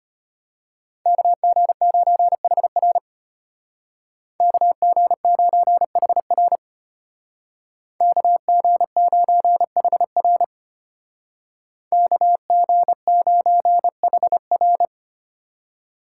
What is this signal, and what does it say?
Morse code, heard as a keyed tone, about 31 wpm, 700 Hz. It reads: KG9HR KG9HR KG9HR KG9HR
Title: Speed Racing - USA 2x2 Call Signs 25wpm